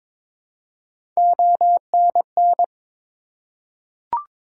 Morse code, heard as a keyed tone, about 21 wpm, 700 Hz. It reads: ONN
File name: practice / 22wpm / ARRL Sections v2 - 22wpm